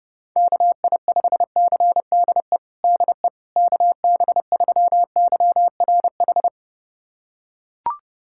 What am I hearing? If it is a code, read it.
Morse code, heard as a keyed tone, about 30 wpm, 700 Hz. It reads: KI5CDE DE KB3YRH
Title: USA Callsign DE Callsign 30wpm